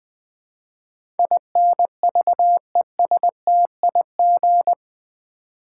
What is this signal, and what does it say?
Morse code, heard as a keyed tone, about 20 wpm, 700 Hz. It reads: INVESTIG
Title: Build A Word - 4 Letters At A Time - 12-Letter Words - 20wpm